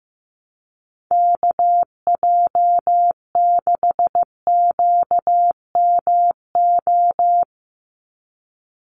Morse code, heard as a keyed tone, about 15 wpm, 700 Hz. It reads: KJ6QMO
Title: USA Call Signs 15wpm